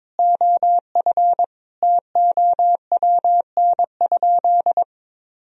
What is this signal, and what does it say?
Morse code, heard as a keyed tone, about 22 wpm, 700 Hz. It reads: OF TOWN?
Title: Sentences from Top 1000 Words - Review Part 3 - 22wpm